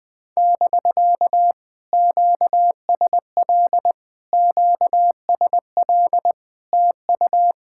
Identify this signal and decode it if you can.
Morse code, heard as a keyed tone, about 20 wpm, 700 Hz. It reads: <BK> QSL QSL TU
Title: POTA QSO Practice - 20wpm